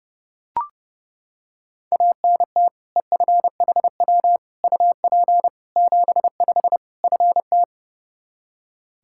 Morse code, heard as a keyed tone, about 29 wpm, 700 Hz. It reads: ANT EFHW UP 75 FT
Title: Practice QSO - Exchange 2 - Info - 30wpm